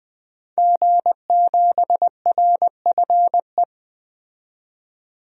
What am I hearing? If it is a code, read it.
Morse code, heard as a keyed tone, about 20 wpm, 700 Hz. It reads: G7RFE
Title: DXCC-100 Call Signs - 5 Characters - 20wpm